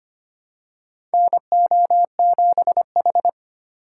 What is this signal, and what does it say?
Morse code, heard as a keyed tone, about 25 wpm, 700 Hz. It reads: NO7H
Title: USA Call Signs - 4 Characters - 3x Character Spacing - 25wpm